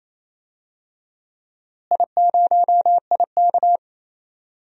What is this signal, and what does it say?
Morse code, heard as a keyed tone, about 28 wpm, 700 Hz. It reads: I0IK
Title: DXCC-100 Call Signs - 4 Characters - Repeated 2x - 28wpm